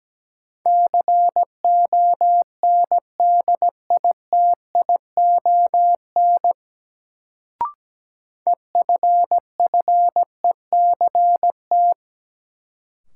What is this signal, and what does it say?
Morse code, heard as a keyed tone, about 17 wpm, 700 Hz. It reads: CONDITION EFFECT
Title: Sentences from Top 1000 Words - New Word Drill - 17wpm